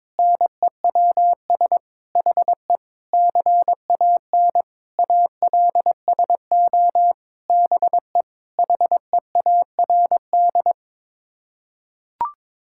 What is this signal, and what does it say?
Morse code, heard as a keyed tone, about 21 wpm, 700 Hz. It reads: NEWS HE CAN ALSO BE HEARD
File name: Sets of 9 Words - 22wpm